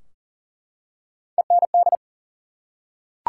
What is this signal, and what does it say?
Morse code, heard as a keyed tone, about 40 wpm, 700 Hz. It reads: END
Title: Three Letter Words - 40wpm